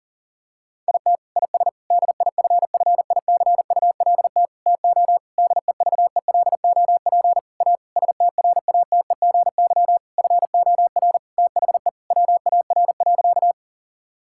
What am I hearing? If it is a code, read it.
Morse code, heard as a keyed tone, about 40 wpm, 700 Hz. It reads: IT IS DIFFICULT TO DEVELOP A STRATEGY FOR THE WAR.
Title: Sentences from Top 1000 Words - Review Part 5 - 40wpm